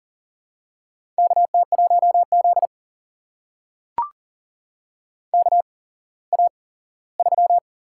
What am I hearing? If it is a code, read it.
Morse code, heard as a keyed tone, about 39 wpm, spaced as for 19 wpm, 700 Hz. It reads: KT1Z KA3
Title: USA Call Signs - 4 Characters - 8x Character Spacing - 40wpm